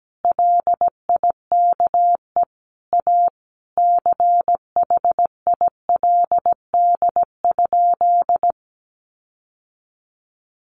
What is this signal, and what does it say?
Morse code, heard as a keyed tone, about 17 wpm, 700 Hz. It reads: LIKE A CHILD?
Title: Sentences from Top 1300 Words 17wpm